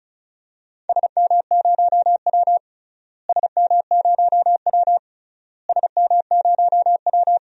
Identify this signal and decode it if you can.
Morse code, heard as a keyed tone, about 34 wpm, 700 Hz. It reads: SM0W SM0W SM0W
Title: DXCC-100 Call Signs - 4 Characters - Repeated 3x - 35wpm